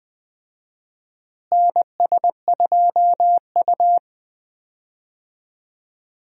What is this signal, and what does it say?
Morse code, heard as a keyed tone, about 21 wpm, 700 Hz. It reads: NS2U
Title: USA Call Signs - 4 Characters - Repeated 3x - 20wpm